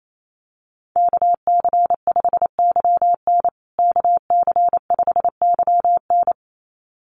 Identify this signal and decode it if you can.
Morse code, heard as a keyed tone, about 28 wpm, 700 Hz. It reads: KC5YN KC5YN
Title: USA Call Signs Repeated 2x 28wpm